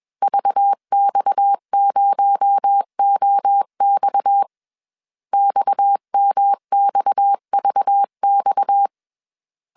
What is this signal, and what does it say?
Morse code, heard as a keyed tone, about 21 wpm, 800 Hz. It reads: VX0OX XMXVX
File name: CW / DET / 1 6 1500C S C 20wpm 2116